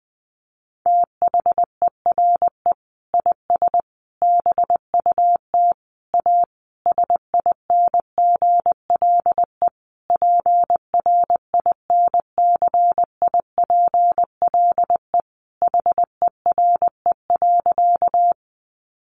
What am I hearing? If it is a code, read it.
Morse code, heard as a keyed tone, about 19 wpm, 700 Hz. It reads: THERE IS BUT A SINGLE PRINCIPLE HERE.
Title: Sentences from Top 1400 Words 20wpm